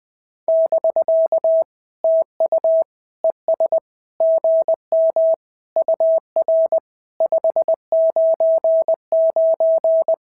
Morse code, heard as a keyed tone, about 20 wpm, 650 Hz. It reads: <BK> TU ES GM UR 599
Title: SOTA QSO Practice 20wpm